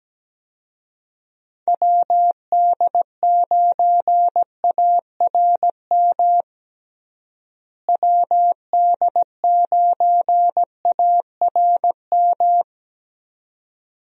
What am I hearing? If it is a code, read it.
Morse code, heard as a keyed tone, about 17 wpm, 700 Hz. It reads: WD9ARM WD9ARM
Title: USA Call Signs - 6 Characters - Repeated 2x - 17wpm